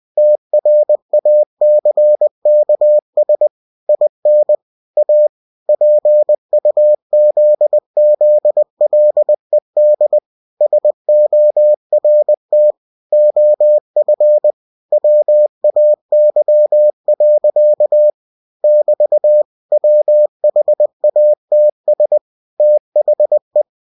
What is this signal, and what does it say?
Morse code, heard as a keyed tone, about 20 wpm, 600 Hz. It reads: TRACKS IN A PUZZLED SORT OF WAY. <BT> WHATS THE
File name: cw-books / winnie-the-pooh / winnie-20wpm-page-0022